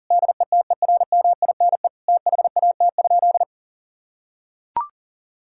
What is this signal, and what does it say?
Morse code, heard as a keyed tone, about 40 wpm, 700 Hz. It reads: DETERMINE THAT?
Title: Sentences from Top 1000 Words - Review Part 5 - 40wpm